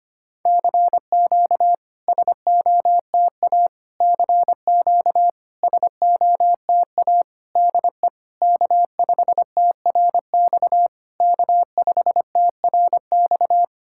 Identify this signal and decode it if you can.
Morse code, heard as a keyed tone, about 25 wpm, 690 Hz. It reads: CQ SOTA CQ SOTA DE K5TRX K5TRX
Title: SOTA QSO Practice 25wpm